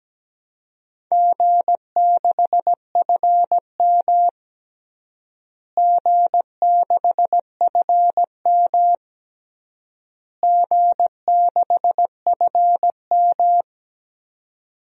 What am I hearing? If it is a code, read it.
Morse code, heard as a keyed tone, about 17 wpm, 700 Hz. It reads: G6FM G6FM G6FM
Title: DXCC-100 Call Signs - 4 Characters - Repeated 3x - 17wpm